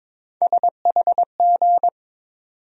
Morse code, heard as a keyed tone, about 22 wpm, 700 Hz. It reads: SHG
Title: USA Call Sign Postfixes - 3 Letters 22wpm